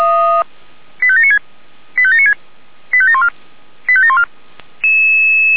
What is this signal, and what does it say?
Five-tone selective call (ZVEI), heard as quick runs of five tones, tones of about 70 ms. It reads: 85796 85796 85714 85714